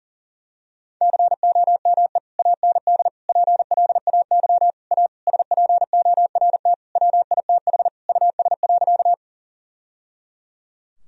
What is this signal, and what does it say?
Morse code, heard as a keyed tone, about 37 wpm, 700 Hz. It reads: COME AND PLAY A SPORT WITH US.
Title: Sentences from Top 1000 Words - Review Part 4 - 40wpm